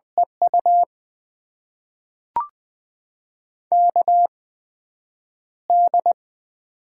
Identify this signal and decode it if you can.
Morse code, heard as a keyed tone, about 20 wpm, spaced as for 9 wpm, 700 Hz. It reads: EU KD
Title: USA Call Signs - 6 Characters - 8x Character Spacing - 20wpm